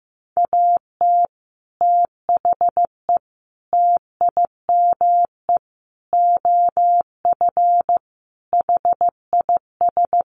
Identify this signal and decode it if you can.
Morse code, heard as a keyed tone, about 15 wpm, 700 Hz. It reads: AT THE TIME OF HIS
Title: Sets of 5 Words Encore2 15wpm